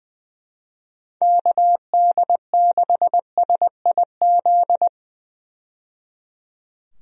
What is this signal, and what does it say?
Morse code, heard as a keyed tone, about 20 wpm, 700 Hz. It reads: KD6SIZ
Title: USA Call Signs - 6 Characters - 20wpm